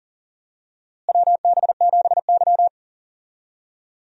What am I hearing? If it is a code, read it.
Morse code, heard as a keyed tone, about 39 wpm, 700 Hz. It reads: WB7Y
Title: USA Call Signs - 4 Characters - Repeated 2x - 40wpm